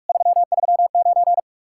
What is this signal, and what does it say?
Morse code, heard as a keyed tone, about 55 wpm, 700 Hz. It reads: 339
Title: Speed-Racing Common QSO Phrases - 45wpm